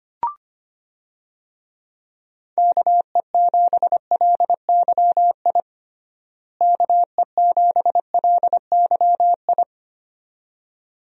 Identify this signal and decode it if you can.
Morse code, heard as a keyed tone, about 25 wpm, 700 Hz. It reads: KE7LYI KE7LYI
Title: USA Call Signs - 6 Characters - Repeated 2x - 25wpm